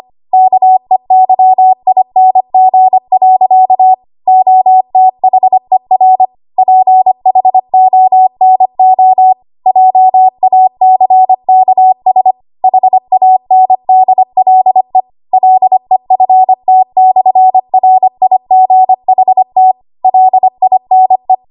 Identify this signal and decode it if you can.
Morse code, heard as a keyed tone, about 25 wpm, 750 Hz. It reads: KEYING. OTHER PHONO JACKS HANDLE LEFT/RIGHT LINE
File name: Archive / 25 WPM / 25WPM